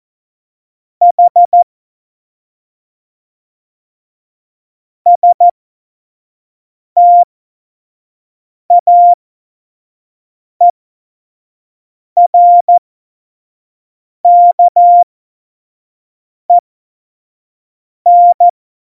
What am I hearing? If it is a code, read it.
Morse code, heard as a keyed tone, about 14 wpm, spaced as for 5 wpm, 700 Hz. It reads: H STAERKEN
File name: Morsekurs / Witze / Witz 28 Juli 2020, 25 F70